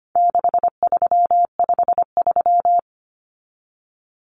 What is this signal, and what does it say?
Morse code, heard as a keyed tone, about 25 wpm, 700 Hz. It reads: 6353
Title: Numbers - 4-Digit 25wpm